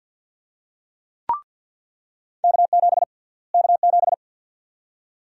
Morse code, heard as a keyed tone, about 49 wpm, 700 Hz. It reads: K7 K7
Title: USA Call Sign Prefixes - Repeated 2x - 50wpm